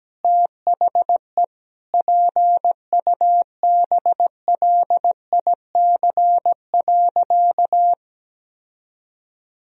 Morse code, heard as a keyed tone, about 17 wpm, 700 Hz. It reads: THE PUBLIC.
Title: Sentences from Top 1000 Words - Review Part 3 - 17wpm